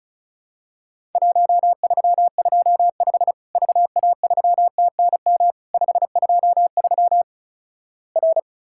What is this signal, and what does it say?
Morse code, heard as a keyed tone, about 34 wpm, 700 Hz. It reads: 1325 VA3TNM 523 R
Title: Worked All Europe DX Contest - QTC Exchange - 35wpm